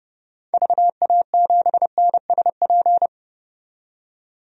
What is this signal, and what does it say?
Morse code, heard as a keyed tone, about 30 wpm, 700 Hz. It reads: VA7NSP
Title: DXCC-100 Call Signs - 6 Characters - 30wpm